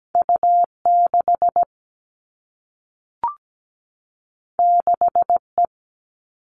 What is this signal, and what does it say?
Morse code, heard as a keyed tone, about 17 wpm, 700 Hz. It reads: U6 6E
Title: All Cut Numbers - 2-Digit 17wpm